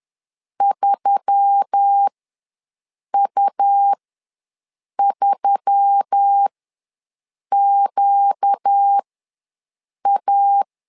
Morse code, heard as a keyed tone, about 10 wpm, spaced as for 6 wpm, 800 Hz. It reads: 3U3QA
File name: LEZIONI / 4 Q1S3 ws 10wpm 2437